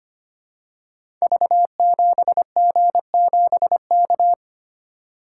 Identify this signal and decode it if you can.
Morse code, heard as a keyed tone, about 25 wpm, 700 Hz. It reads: V7G7K
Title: Code Groups - 5 Letter:Numbers - 25wpm